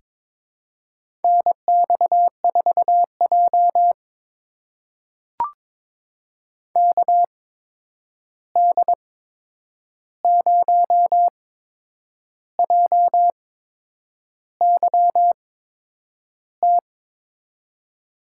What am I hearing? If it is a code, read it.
Morse code, heard as a keyed tone, about 22 wpm, spaced as for 8 wpm, 700 Hz. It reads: NX4J KD0JYT
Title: USA Call Signs - 8x Character Spacing - 22wpm